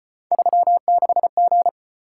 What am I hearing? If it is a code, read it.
Morse code, heard as a keyed tone, about 33 wpm, 700 Hz. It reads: 36G